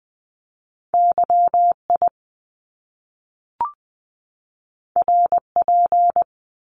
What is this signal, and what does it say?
Morse code, heard as a keyed tone, about 20 wpm, 700 Hz. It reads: YI RP
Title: Top Two Letter Combinations in Words 20wpm